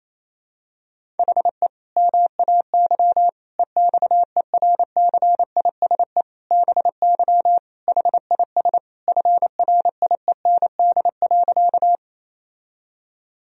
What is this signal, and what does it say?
Morse code, heard as a keyed tone, about 28 wpm, 700 Hz. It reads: HE MAY EXERCISE BY HIS FRIEND.
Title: Sentences from Top 1000 Words - Review Part 4 - 28wpm